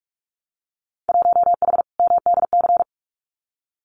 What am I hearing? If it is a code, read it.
Morse code, heard as a keyed tone, about 46 wpm, 700 Hz. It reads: 1H MDC
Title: Winter Field Day Exchanges 45wpm